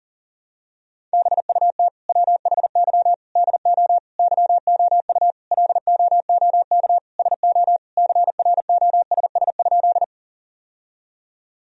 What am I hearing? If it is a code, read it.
Morse code, heard as a keyed tone, about 40 wpm, 700 Hz. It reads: BUT WHY DO YOU LOOK SO CROSS?
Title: Sentences from Top 1000 Words - Review Part 1 - 40wpm